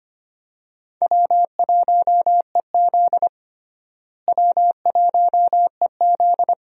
Morse code, heard as a keyed tone, about 25 wpm, 700 Hz. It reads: W1EZ W1EZ
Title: USA Call Signs - 4 Characters - Repeated 2x - 25wpm